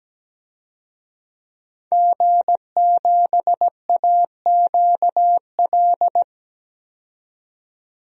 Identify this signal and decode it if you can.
Morse code, heard as a keyed tone, about 17 wpm, 700 Hz. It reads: G7AQL
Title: DXCC-100 Call Signs - 5 Characters - 17wpm